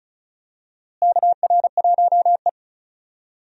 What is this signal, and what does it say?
Morse code, heard as a keyed tone, about 36 wpm, 700 Hz. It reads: KR1E
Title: USA Call Signs - 4 Characters - Repeated 2x - 35wpm